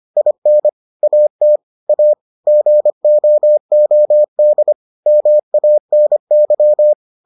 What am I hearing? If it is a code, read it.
Morse code, heard as a keyed tone, about 25 wpm, 600 Hz. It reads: IN AT A GOOD MANY